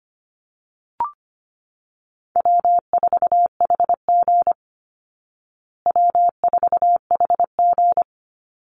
Morse code, heard as a keyed tone, about 25 wpm, 700 Hz. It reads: W4HG W4HG
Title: Easy Call Signs Repeated 3x - 25wpm